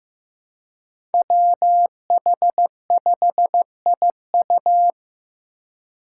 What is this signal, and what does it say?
Morse code, heard as a keyed tone, about 15 wpm, 700 Hz. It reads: WH5IU
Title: Five Challenging Letter-Number Combinations 15wpm